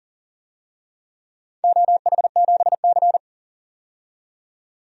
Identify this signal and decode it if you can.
Morse code, heard as a keyed tone, about 40 wpm, 700 Hz. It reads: OH7C
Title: DXCC-100 Call Signs - 4 Characters - 40wpm